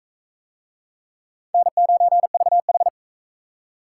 Morse code, heard as a keyed tone, about 44 wpm, 700 Hz. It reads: N9VH